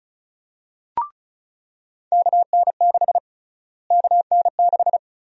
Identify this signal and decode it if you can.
Morse code, heard as a keyed tone, about 36 wpm, 700 Hz. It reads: KN6 KN6
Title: USA Call Sign Prefixes - Repeated 2x - 35wpm